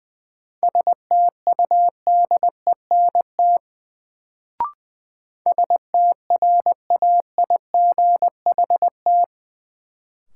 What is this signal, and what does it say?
Morse code, heard as a keyed tone, about 20 wpm, 700 Hz. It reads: STUDENT STRAIGHT
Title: Sentences from Top 1000 Words - New Word Drill - 20wpm